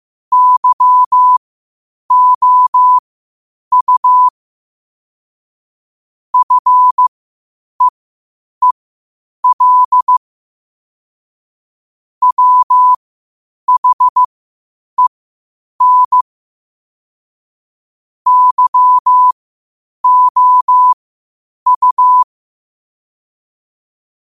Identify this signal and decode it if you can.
Morse code, heard as a keyed tone, about 15 wpm, spaced as for 8 wpm, 1000 Hz. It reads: YOU FEEL WHEN YOU
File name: qotdfiles / 8wpm